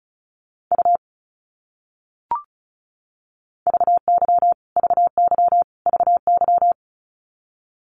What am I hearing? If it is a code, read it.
Morse code, heard as a keyed tone, about 34 wpm, 700 Hz. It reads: U VY VY VY
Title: Abbreviations Repeated 3x - 35wpm